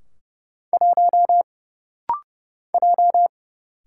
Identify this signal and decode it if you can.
Morse code, heard as a keyed tone, about 30 wpm, 700 Hz. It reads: 1 J
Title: Single Letter-Number - Rapid-Fire 30wpm